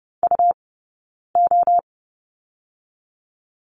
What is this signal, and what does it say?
Morse code, heard as a keyed tone, about 32 wpm, 700 Hz. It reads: U O
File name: Lesson 19d - Letter w - Two Characters 30wpm